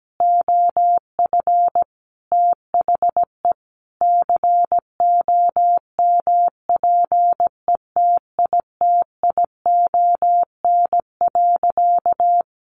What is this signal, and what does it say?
Morse code, heard as a keyed tone, about 17 wpm, 700 Hz. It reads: OF THE COMPETITION.
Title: Sentences from Top 1000 Words 17wpm